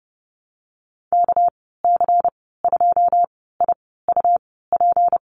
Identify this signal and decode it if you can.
Morse code, heard as a keyed tone, about 30 wpm, spaced as for 17 wpm, 700 Hz. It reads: KC2IUP
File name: Lesson 37g - Letter x - Call Signs 30wpm